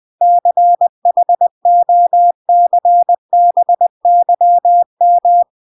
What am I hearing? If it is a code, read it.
Morse code, heard as a keyed tone, about 20 wpm, 700 Hz. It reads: CHOCBYM